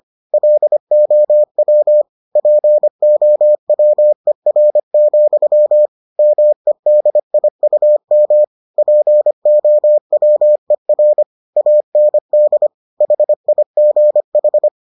Morse code, heard as a keyed tone, about 25 wpm, 600 Hz. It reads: LOW POWER, MEDIUM POWER AND HIGH